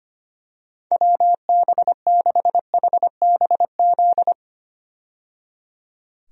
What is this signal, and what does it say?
Morse code, heard as a keyed tone, about 25 wpm, 700 Hz. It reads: WB6HBZ
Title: USA Call Signs - 6 Characters - 25wpm